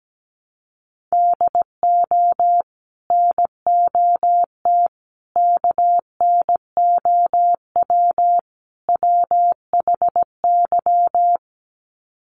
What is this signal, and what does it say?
Morse code, heard as a keyed tone, about 17 wpm, 700 Hz. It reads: DO NOT KNOW WHY
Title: Sets of 4 Words 17wpm